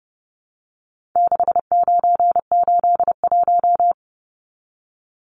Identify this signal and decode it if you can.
Morse code, heard as a keyed tone, about 29 wpm, 700 Hz. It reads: 6981
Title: Numbers - 4-Digit 30wpm